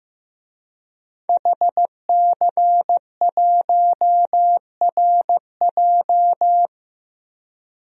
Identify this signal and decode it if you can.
Morse code, heard as a keyed tone, about 15 wpm, 700 Hz. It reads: HC1RJ
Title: DXCC-100 Call Signs - 5 Characters - Repeated 3x - 15wpm